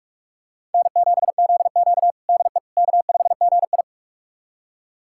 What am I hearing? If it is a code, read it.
Morse code, heard as a keyed tone, about 46 wpm, 700 Hz. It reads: N7ZQ DE K5GI
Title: USA Callsign DE Callsign - 4 Characters - 45wpm